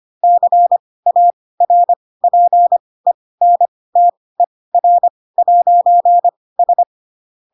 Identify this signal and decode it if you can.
Morse code, heard as a keyed tone, about 25 wpm, spaced as for 18 wpm, 700 Hz. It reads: CARPENTER'S
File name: Pinocchio / Pinocchio ch2 12 25 ik1hge